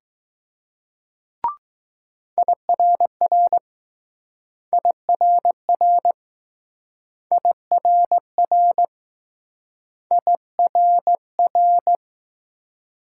Morse code, heard as a keyed tone, about 20 wpm, 700 Hz. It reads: IRR IRR IRR IRR